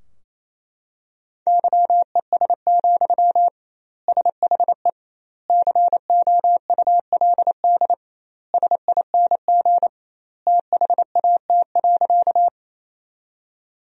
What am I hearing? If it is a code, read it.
Morse code, heard as a keyed tone, about 27 wpm, 700 Hz. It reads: YES, SHE COULD SING THAT.
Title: 2x Word-Spacing - Sentences from Top 500 Words - 28wpm